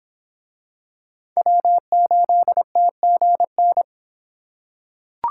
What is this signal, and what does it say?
Morse code, heard as a keyed tone, about 26 wpm, 700 Hz. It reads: W8TGN